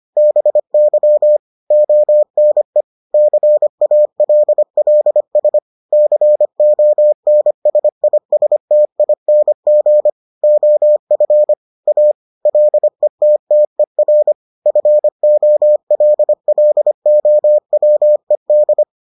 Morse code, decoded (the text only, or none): BY ONE CALLS CONSISTING OF A LETTER FOLLOWED